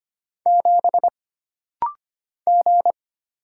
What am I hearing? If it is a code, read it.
Morse code, heard as a keyed tone, about 25 wpm, 700 Hz. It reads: 7 G
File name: Single Character (Letter:Number:Punctuation:Prosign) - Rapid-Fire - 25wpm